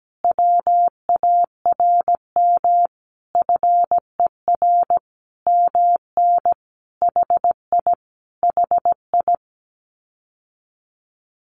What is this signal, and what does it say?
Morse code, heard as a keyed tone, about 17 wpm, 700 Hz. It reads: WARM FER MN HI HI
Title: Common QSO Phrases Encore 17wpm